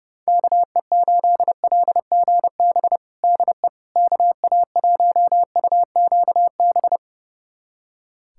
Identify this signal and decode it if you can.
Morse code, heard as a keyed tone, about 32 wpm, 700 Hz. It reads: KE8LGB DE KA1UQB
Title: USA Callsign DE Callsign - 6 Characters - 30wpm